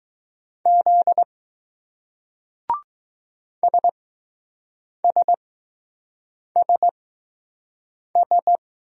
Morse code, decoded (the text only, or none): Z SSSS